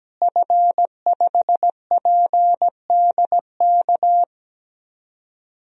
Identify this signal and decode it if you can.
Morse code, heard as a keyed tone, about 17 wpm, 700 Hz. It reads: F5PDK
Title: DXCC-100 Call Signs - 5 Characters - Repeated 3x - 17wpm